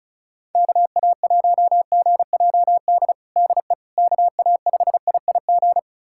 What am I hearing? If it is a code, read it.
Morse code, heard as a keyed tone, about 36 wpm, 690 Hz. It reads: KA1GJD DE KA5IIG
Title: USA Callsign DE Callsign - 6 Characters - 35wpm